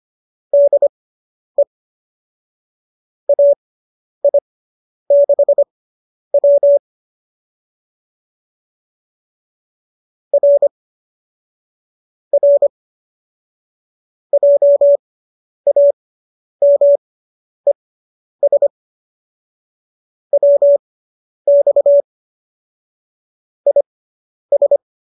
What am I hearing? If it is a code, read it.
Morse code, heard as a keyed tone, about 25 wpm, spaced as for 10 wpm, 570 Hz. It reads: DE AI6W R R JAMES WX IS